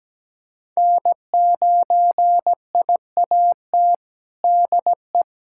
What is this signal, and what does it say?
Morse code, heard as a keyed tone, about 17 wpm, 700 Hz. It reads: N9IAT DE
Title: USA Callsign DE Callsign - 5 Characters - 17wpm